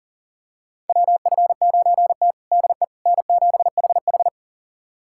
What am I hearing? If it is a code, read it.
Morse code, heard as a keyed tone, about 40 wpm, 700 Hz. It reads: WF9T DE N7HH
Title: USA Callsign DE Callsign - 4 Characters - 40wpm